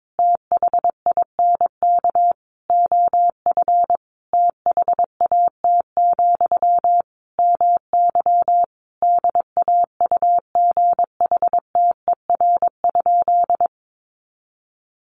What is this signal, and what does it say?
Morse code, heard as a keyed tone, about 22 wpm, 700 Hz. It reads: THINK OF THAT, MY DAUGHTER?
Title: Sentences from Top 1500 Words 22wpm